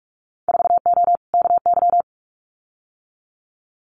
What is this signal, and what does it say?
Morse code, heard as a keyed tone, about 46 wpm, 700 Hz. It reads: 4O KY
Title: Winter Field Day Exchanges Encore 45wpm